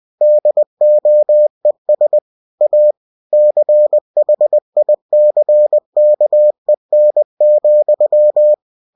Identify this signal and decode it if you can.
Morse code, heard as a keyed tone, about 20 wpm, 600 Hz. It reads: DOES A CHICKEN,
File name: cw-books / winnie-the-pooh / winnie-20wpm-page-0046